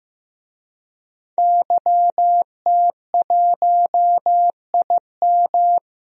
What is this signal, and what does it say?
Morse code, heard as a keyed tone, about 15 wpm, 700 Hz. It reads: YT1IM
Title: DXCC-100 Call Signs - 5 Characters - Repeated 3x - 15wpm